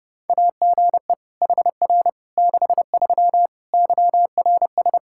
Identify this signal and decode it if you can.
Morse code, heard as a keyed tone, about 32 wpm, 700 Hz. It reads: AGE HR 63 YRS
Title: K1USN SST QSO Practice 30wpm